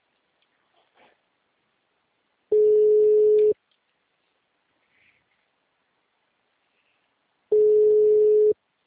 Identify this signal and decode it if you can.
Ringback tone (Europe, 425 Hz).
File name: out- -2032- - - .619